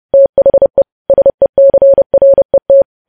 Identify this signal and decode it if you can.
Morse code, heard as a keyed tone, about 30 wpm, 550 Hz. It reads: THE SECRET
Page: news headlines in morse code at 30 wpm